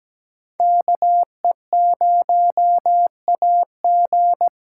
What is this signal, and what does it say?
Morse code, heard as a keyed tone, about 17 wpm, 700 Hz. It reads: KE0AG
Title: USA Call Signs - 8x Character Spacing - 17wpm